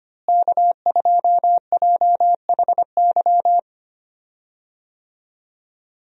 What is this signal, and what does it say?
Morse code, heard as a keyed tone, about 25 wpm, 700 Hz. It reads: K2JHY
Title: USA Call Signs - 5 Characters - Repeated 2x - 25wpm